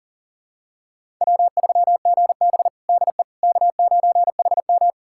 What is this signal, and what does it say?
Morse code, heard as a keyed tone, about 37 wpm, 700 Hz. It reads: W3GB DE K9HM
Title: USA Callsign DE Callsign - 4 Characters - 40wpm